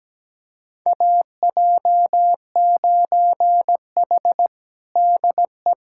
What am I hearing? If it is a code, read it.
Morse code, heard as a keyed tone, about 17 wpm, 700 Hz. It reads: AJ9H DE
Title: USA Callsign DE Callsign - 4 Characters - 17wpm